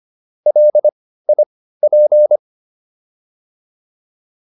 Morse code, heard as a keyed tone, about 25 wpm, spaced as for 15 wpm, 600 Hz. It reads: LIP